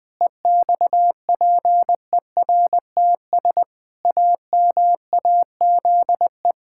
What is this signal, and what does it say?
Morse code, heard as a keyed tone, about 20 wpm, 700 Hz. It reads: EXPERTS AMAZE
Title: Pangram Challenge v2 - 20wpm